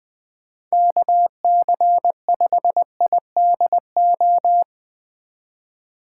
Morse code, heard as a keyed tone, about 20 wpm, 700 Hz. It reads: KC5IDO
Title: USA Call Signs - 6 Characters - 4x Character Spacing - 20wpm